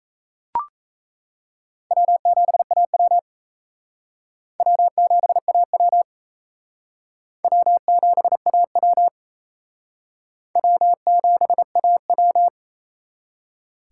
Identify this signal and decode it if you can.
Morse code, heard as a keyed tone, about 35 wpm, 700 Hz. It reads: W7AW W7AW W7AW W7AW
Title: Speed Racing - USA 1x2 Call Signs 28wpm